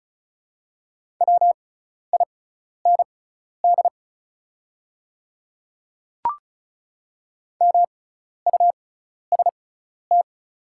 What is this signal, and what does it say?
Morse code, heard as a keyed tone, about 36 wpm, spaced as for 12 wpm, 690 Hz. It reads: WIND MUST
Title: 6x Character Spacing - Top 500 Words - 35wpm